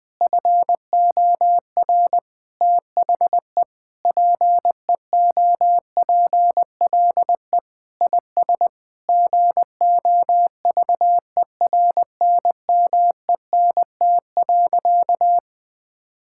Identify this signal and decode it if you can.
Morse code, heard as a keyed tone, about 21 wpm, 700 Hz. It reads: FOR THE PEOPLE IS GOVERNMENT.
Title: Sentences from Top 1000 Words - Review Part 6 - 20wpm